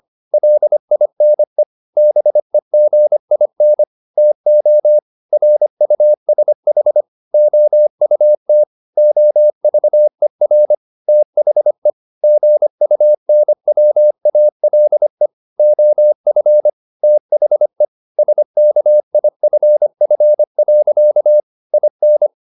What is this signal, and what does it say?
Morse code, heard as a keyed tone, about 25 wpm, 600 Hz. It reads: LINE BEGIN TO RUSH OUT OVER THE GUNWALE OF THE SKIFF. IN
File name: the old man and the sea part 1 25wpm